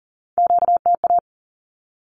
Morse code, heard as a keyed tone, about 40 wpm, 700 Hz. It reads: QTA
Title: Q-Codes 40wpm